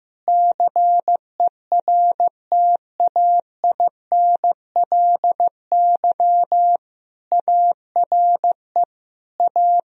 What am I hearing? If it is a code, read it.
Morse code, heard as a keyed tone, about 15 wpm, 700 Hz. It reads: CERTAINLY ARE A
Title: Sentences from Top 1000 Words - Review Part 3 - 15wpm